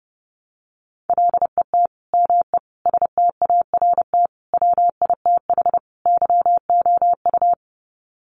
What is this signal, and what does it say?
Morse code, heard as a keyed tone, about 30 wpm, 700 Hz. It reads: LET ME START WITH YOU
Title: Sets of 5 Words Encore 30wpm